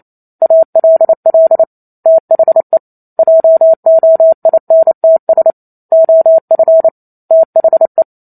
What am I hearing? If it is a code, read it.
Morse code, heard as a keyed tone, about 28 wpm, 650 Hz. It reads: ALL THE JOINTS OF THE
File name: Chapter 2 - 28 WPM